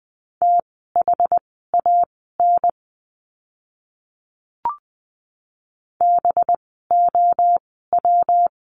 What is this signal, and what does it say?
Morse code, heard as a keyed tone, about 20 wpm, spaced as for 15 wpm, 700 Hz. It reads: THAN BOW
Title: 185 Lesson 30f - Letter b - Words 20wpm